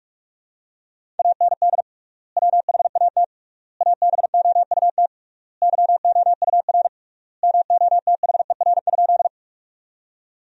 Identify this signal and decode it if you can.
Morse code, heard as a keyed tone, about 42 wpm, 700 Hz. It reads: AND WHAT ABOUT YOUR MOTHER?
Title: 3x Word-Spacing - Sentences from Top 200 Words - 45wpm